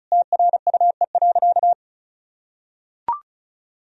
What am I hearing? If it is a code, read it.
Morse code, heard as a keyed tone, about 34 wpm, 700 Hz. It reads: TRUE.
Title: Sentences from Top 1000 Words - Review Part 4 - 35wpm